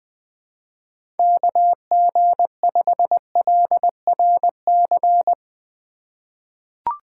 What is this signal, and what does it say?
Morse code, heard as a keyed tone, about 20 wpm, 700 Hz. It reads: KG5LRC
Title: USA Call Signs - 6 Characters - 3x Character Spacing - 20wpm